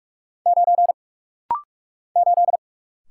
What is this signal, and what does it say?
Morse code, heard as a keyed tone, about 46 wpm, 700 Hz. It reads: 9 8
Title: Single Letter-Number - Rapid-Fire 45wpm